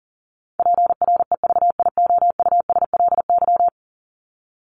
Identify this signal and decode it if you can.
Morse code, heard as a keyed tone, about 40 wpm, 700 Hz. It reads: PREVIOUSLY
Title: LY Suffix - Top 100 Words with Base Words 40wpm